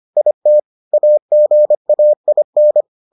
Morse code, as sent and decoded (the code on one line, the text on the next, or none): .. - / .- --. .- .. -.
IT AGAIN